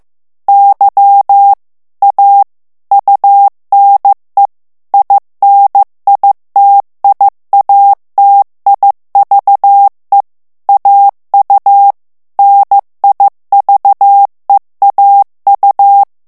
Morse code, heard as a keyed tone, about 15 wpm, 800 Hz. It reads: Y A UNE INITIATIVE AU NIVEAU
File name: morse mp3 / FRANCAIS TEXTE CLAIR 1 VITESSE 14 MOTS MINUTES